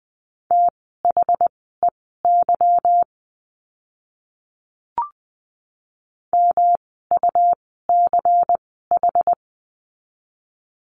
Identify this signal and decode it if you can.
Morse code, heard as a keyed tone, about 20 wpm, spaced as for 14 wpm, 700 Hz. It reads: THEY MUCH
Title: 192 Lesson 31f - Letter v - Words 20wpm